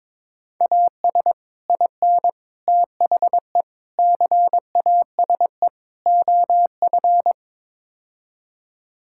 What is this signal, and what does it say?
Morse code, heard as a keyed tone, about 21 wpm, 700 Hz. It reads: AS IN THE CASE OF
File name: practice / 22wpm / Build A Phrase - 5-Word Phrases - 22wpm